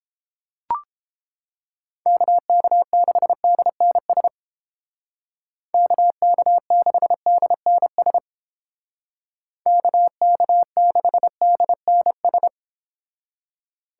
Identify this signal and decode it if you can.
Morse code, heard as a keyed tone, about 31 wpm, 700 Hz. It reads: KK6DNS KK6DNS KK6DNS